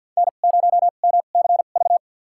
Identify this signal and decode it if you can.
Morse code, heard as a keyed tone, about 52 wpm, spaced as for 36 wpm, 700 Hz. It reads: N0MCV
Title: USA Call Signs - 2x Character Spacing - 50wpm